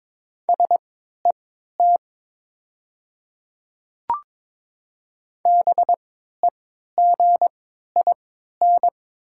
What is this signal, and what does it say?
Morse code, heard as a keyed tone, about 22 wpm, spaced as for 12 wpm, 690 Hz. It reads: SET BEGIN
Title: 3x Character Spacing - Top 300 Words - 22wpm